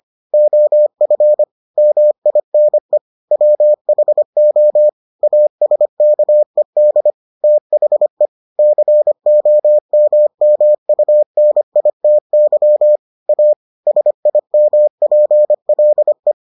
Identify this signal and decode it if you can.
Morse code, heard as a keyed tone, about 25 wpm, 600 Hz. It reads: OF MINE WHO ASKED THE COMMUNITY A SIMPLE